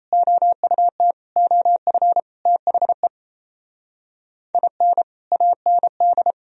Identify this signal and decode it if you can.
Morse code, heard as a keyed tone, about 37 wpm, 700 Hz. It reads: OUT OF THE IN AND